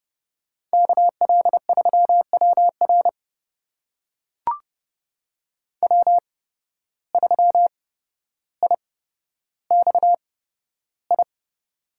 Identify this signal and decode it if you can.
Morse code, heard as a keyed tone, about 30 wpm, spaced as for 13 wpm, 700 Hz. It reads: KL3WR W3IXI
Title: USA Call Signs - 5 Characters - 8x Character Spacing - 30wpm